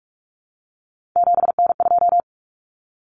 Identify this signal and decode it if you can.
Morse code, heard as a keyed tone, about 44 wpm, 700 Hz. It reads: 7N2
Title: Common Cut Numbers - 3-Digit 45wpm